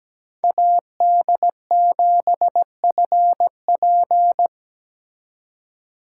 Morse code, heard as a keyed tone, about 20 wpm, 700 Hz. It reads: AD7FP
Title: Speed Racing - USA Call Signs 17wpm